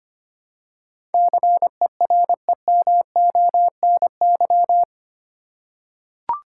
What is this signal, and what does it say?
Morse code, heard as a keyed tone, about 25 wpm, 700 Hz. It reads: CEREMONY
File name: Eight Letter Words - 25wpm